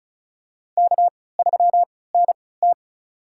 Morse code, heard as a keyed tone, about 34 wpm, spaced as for 20 wpm, 700 Hz. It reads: K3NT
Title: USA Call Signs - 3x Character Spacing - 35wpm